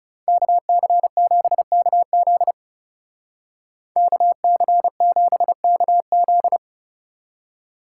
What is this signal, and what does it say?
Morse code, heard as a keyed tone, about 37 wpm, 700 Hz. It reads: KC7KZ KC7KZ
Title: Speed Racing - USA Call Signs 30wpm